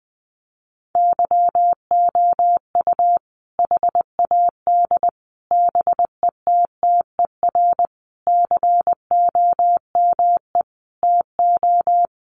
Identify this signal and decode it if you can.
Morse code, heard as a keyed tone, about 20 wpm, 700 Hz. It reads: YOU HAD BETTER COME TO
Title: Sentences from Top 400 Words 20wpm